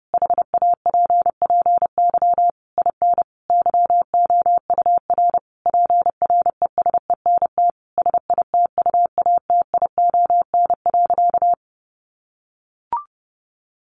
Morse code, heard as a keyed tone, about 30 wpm, 700 Hz. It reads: HAPPY IN YOUR PRESENT SITUATION.
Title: Sentences from Top 1200 Words 30wpm